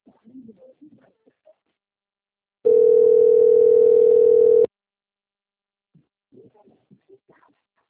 Ringback tone (North America), 440 + 480 Hz.